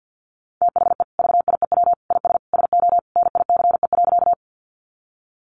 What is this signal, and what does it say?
Morse code, heard as a keyed tone, about 50 wpm, 700 Hz. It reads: THE VIEW IS SO NICE.
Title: Additional Easy Phrases 50wpm